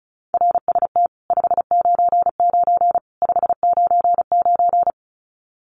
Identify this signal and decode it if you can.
Morse code, heard as a keyed tone, about 34 wpm, 700 Hz. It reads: RST 599 599
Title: Common QSO Phrases 35wpm